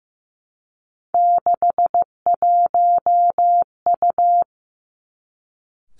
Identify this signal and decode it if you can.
Morse code, heard as a keyed tone, about 15 wpm, 700 Hz. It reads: 61U
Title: Three Challenging Letter-Number Combinations 15wpm